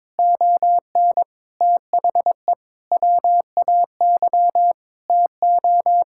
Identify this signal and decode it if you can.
Morse code, heard as a keyed tone, about 21 wpm, 700 Hz. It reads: ON THE WAY TO
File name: practice / 22wpm / Sets of 8 Words - 22wpm